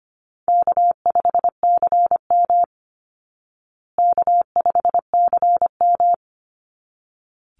Morse code, heard as a keyed tone, about 25 wpm, 700 Hz. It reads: K5CM K5CM
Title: Easy Call Signs Repeated 2x - 25wpm